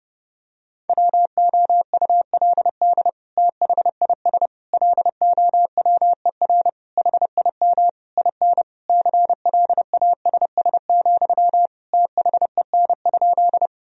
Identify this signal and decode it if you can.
Morse code, heard as a keyed tone, about 30 wpm, 700 Hz. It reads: WOULD THIS LOWER HIM IN CLASS, THEN?
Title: Sentences from Top 1000 Words - Review Part 4 - 30wpm